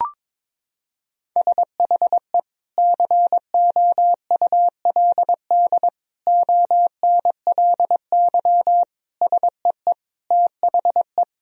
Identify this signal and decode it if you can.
Morse code, heard as a keyed tone, about 22 wpm, 700 Hz. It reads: SHE COULD ONLY SEE THE
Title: Sentences from Top 1000 Words - Review Part 2 - 22wpm